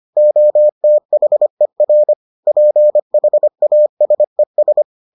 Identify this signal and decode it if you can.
Morse code, heard as a keyed tone, about 25 wpm, 600 Hz. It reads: OTHER PHASES